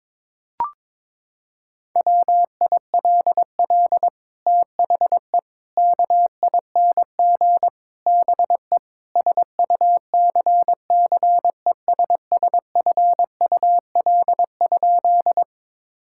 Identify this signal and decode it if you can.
Morse code, heard as a keyed tone, about 22 wpm, 700 Hz. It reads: WILL THE KING BE SUCCESSFUL?
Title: Sentences from Top 1000 Words - Review Part 6 - 22wpm